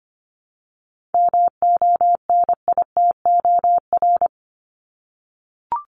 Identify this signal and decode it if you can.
Morse code, heard as a keyed tone, about 25 wpm, 700 Hz. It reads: MONITOR
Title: Top 1751-2000 Words 25wpm